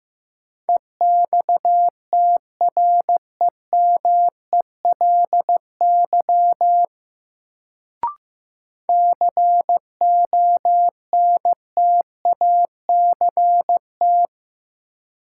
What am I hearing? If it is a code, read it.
Morse code, heard as a keyed tone, about 15 wpm, 700 Hz. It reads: EXTREMELY CONTACT
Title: Sentences from Top 1000 Words - Review Part 7 - New Word Drill - 15wpm